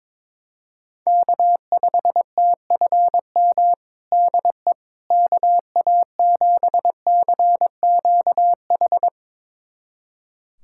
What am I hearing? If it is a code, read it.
Morse code, heard as a keyed tone, about 22 wpm, 700 Hz. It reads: K5TFM DE KA7CQH
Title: USA Callsign DE Callsign 22wpm